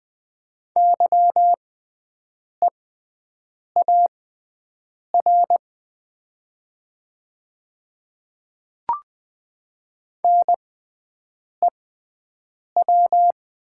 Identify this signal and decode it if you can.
Morse code, heard as a keyed tone, about 20 wpm, spaced as for 7 wpm, 700 Hz. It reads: YEAR NEW
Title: 6x Character Spacing - Top 200 Words - 20wpm